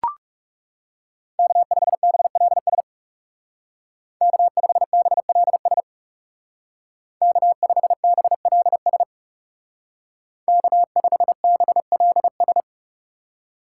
Morse code, heard as a keyed tone, about 37 wpm, 700 Hz. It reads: K5BLS K5BLS K5BLS K5BLS
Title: Speed Racing - USA 1x3 Call Signs 30wpm